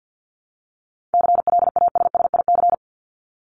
Morse code, heard as a keyed tone, about 50 wpm, 700 Hz. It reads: CLASSIC
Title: Top 751-1000 Words 50wpm